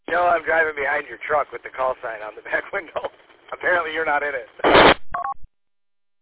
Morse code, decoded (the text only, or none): none